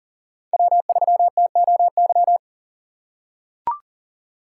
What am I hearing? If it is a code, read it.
Morse code, heard as a keyed tone, about 40 wpm, 700 Hz. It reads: W3TOY
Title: USA Call Signs - 5 Characters - 4x Character Spacing - 40wpm